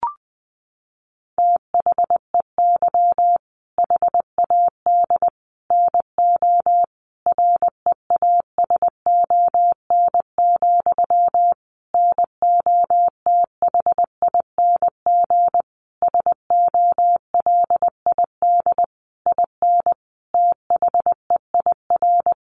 Morse code, heard as a keyed tone, about 20 wpm, 700 Hz. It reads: THEY HAD NO REASON, NOTHING SOLID IN THEIR
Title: Sentences from Top 1500 Words 20wpm